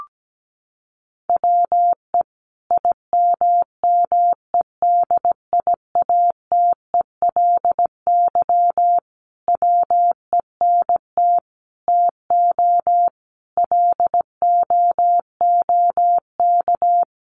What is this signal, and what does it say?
Morse code, heard as a keyed tone, about 17 wpm, 700 Hz. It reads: WE IMMEDIATELY WENT TO LOOK
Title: Sentences from Top 1100 Words 17wpm